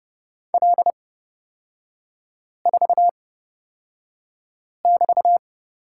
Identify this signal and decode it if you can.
Morse code, heard as a keyed tone, about 30 wpm, spaced as for 5 wpm, 700 Hz. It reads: L4<BT>
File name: Lesson 36e - BT - Three Characters 30wpm